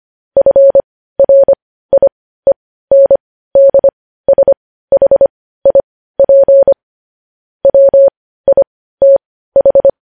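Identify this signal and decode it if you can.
Morse code, heard as a keyed tone, about 25 wpm, spaced as for 15 wpm, 550 Hz. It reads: FRIENDSHIP WITH
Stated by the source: News headlines sent in Morse code at 15 wpm.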